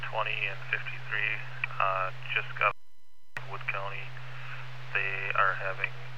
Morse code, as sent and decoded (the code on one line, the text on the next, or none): none